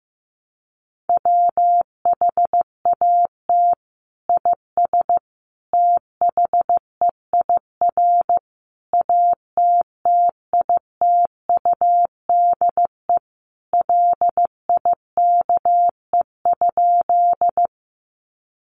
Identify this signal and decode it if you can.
Morse code, heard as a keyed tone, about 15 wpm, 700 Hz. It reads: WHAT IS THEIR ATTITUDE LIKE?
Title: Sentences from Top 1300 Words 15wpm